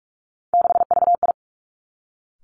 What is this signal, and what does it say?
Morse code, heard as a keyed tone, about 46 wpm, 700 Hz. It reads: 6VI
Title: Three Challenging Letter-Number Combinations 45wpm